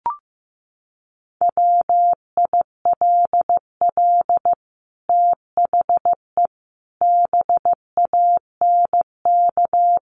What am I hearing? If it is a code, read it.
Morse code, heard as a keyed tone, about 15 wpm, 700 Hz. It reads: WILL THE BANK
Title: Sentences from Top 1500 Words 15wpm